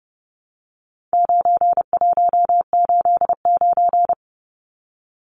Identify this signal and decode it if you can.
Morse code, heard as a keyed tone, about 30 wpm, 700 Hz. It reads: 9189
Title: Numbers - 4-Digit 30wpm